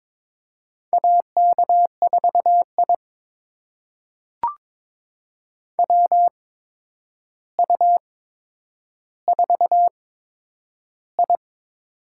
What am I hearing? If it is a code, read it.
Morse code, heard as a keyed tone, about 22 wpm, spaced as for 9 wpm, 700 Hz. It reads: AK4I WU4I
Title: USA Call Signs - 4 Characters - 8x Character Spacing - 22wpm